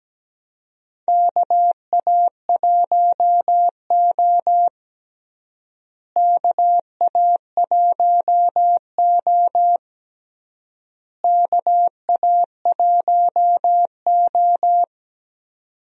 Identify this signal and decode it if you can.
Morse code, heard as a keyed tone, about 17 wpm, 700 Hz. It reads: KA1O KA1O KA1O
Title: USA Call Signs - 4 Characters - Repeated 3x - 17wpm